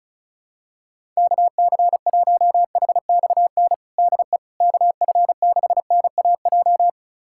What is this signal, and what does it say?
Morse code, heard as a keyed tone, about 34 wpm, 700 Hz. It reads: KC1HXN DE KF6NAJ
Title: USA Callsign DE Callsign - 6 Characters - 35wpm